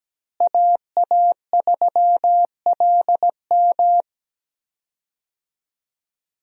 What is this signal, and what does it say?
Morse code, heard as a keyed tone, about 17 wpm, 700 Hz. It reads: AA3LM
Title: USA Call Signs - 5 Characters - Repeated 2x - 17wpm